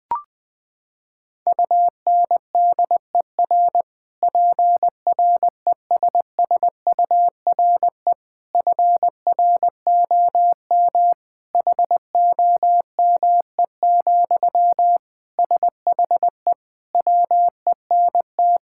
Morse code, heard as a keyed tone, about 20 wpm, 700 Hz. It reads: UNDER PRESSURE FROM HOME, SHE WENT
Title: Sentences from Top 1000 Words - Review Part 6 - 20wpm